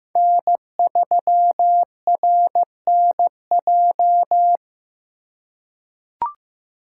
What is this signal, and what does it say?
Morse code, heard as a keyed tone, about 16 wpm, 700 Hz. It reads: N3RNJ
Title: USA Call Signs - 5 Characters - 2x Character Spacing - 15wpm